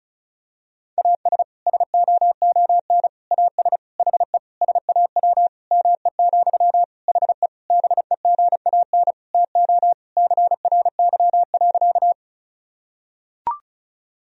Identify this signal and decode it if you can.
Morse code, heard as a keyed tone, about 36 wpm, 700 Hz. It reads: AS SOON AS HE SAW ME, HE BEGAN TO CRY.
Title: Sentences from Top 1000 Words - Review Part 2 - 35wpm